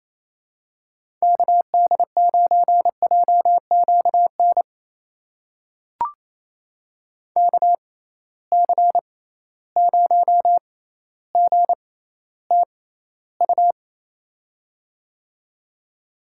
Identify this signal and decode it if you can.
Morse code, heard as a keyed tone, about 28 wpm, spaced as for 14 wpm, 700 Hz. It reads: KD9JQN KC0GTU
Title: USA Call Signs - 6 Characters - 6x Character Spacing - 28wpm